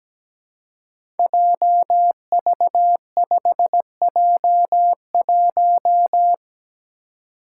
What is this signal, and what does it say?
Morse code, heard as a keyed tone, about 17 wpm, 700 Hz. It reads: JV5J1
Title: Five Challenging Letter-Number Combinations 17wpm